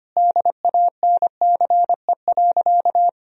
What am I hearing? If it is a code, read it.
Morse code, heard as a keyed tone, about 25 wpm, 700 Hz. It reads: DANCE.
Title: Sentences from Top 1000 Words - Review Part 4 - 25wpm